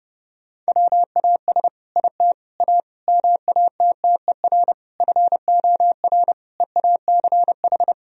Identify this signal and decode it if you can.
Morse code, heard as a keyed tone, about 30 wpm, 700 Hz. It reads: WAS IT A MATTER FOR EACH
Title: Sentences from Top 1000 Words - Review Part 4 - 30wpm